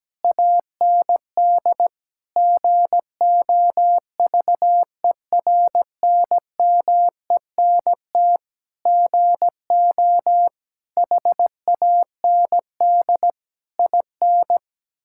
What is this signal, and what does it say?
Morse code, heard as a keyed tone, about 17 wpm, 700 Hz. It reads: AND GOVERNMENT GO HAND IN
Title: Sentences from Top 1000 Words - Review Part 5 - 17wpm